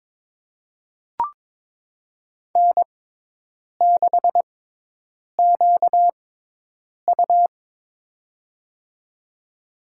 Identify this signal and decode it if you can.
Morse code, heard as a keyed tone, about 22 wpm, spaced as for 8 wpm, 700 Hz. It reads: N6QU
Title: USA Call Signs - 4 Characters - 6x Character Spacing - 22wpm